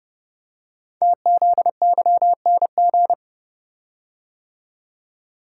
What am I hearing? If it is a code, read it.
Morse code, heard as a keyed tone, about 30 wpm, 700 Hz. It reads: TZYNG
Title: Code Groups - 5 Letter:Numbers - 30wpm